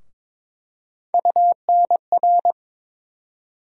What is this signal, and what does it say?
Morse code, heard as a keyed tone, about 22 wpm, 700 Hz. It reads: UNR
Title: USA Call Sign Postfixes - 3 Letters 22wpm